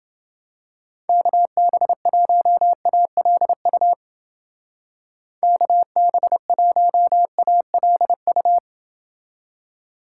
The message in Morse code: KB1ALU KB1ALU

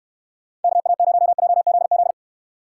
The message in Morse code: KA0JZZ

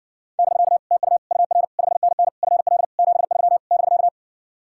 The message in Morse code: <BK> TU UR 5NN FL 73 <BK>

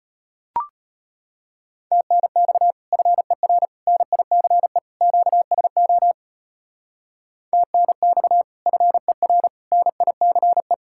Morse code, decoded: TNX FER NICE QSO TNX FER NICE

